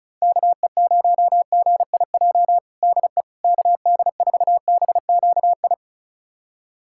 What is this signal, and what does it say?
Morse code, heard as a keyed tone, about 36 wpm, 700 Hz. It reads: KE0GIJ DE KD4BQI
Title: USA Callsign DE Callsign - 6 Characters - 35wpm